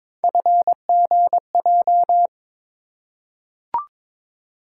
Morse code, heard as a keyed tone, about 22 wpm, 700 Hz. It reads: FGJ
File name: practice / 22wpm / Three Letter-Number Combinations - 22wpm